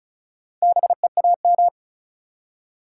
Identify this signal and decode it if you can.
Morse code, heard as a keyed tone, about 36 wpm, 700 Hz. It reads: BEAM
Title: Top 180 QSO Elements 35wpm